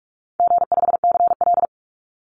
Morse code, heard as a keyed tone, about 44 wpm, 700 Hz. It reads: G5CL
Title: Easy Call Signs Repeated 2x - 45wpm